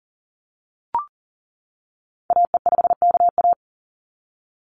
Morse code, heard as a keyed tone, about 40 wpm, 700 Hz. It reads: AE5KA
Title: NA SOTA Activator Callsigns 40wpm